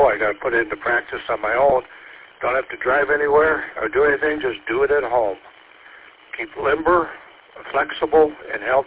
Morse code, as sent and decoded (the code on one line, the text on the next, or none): none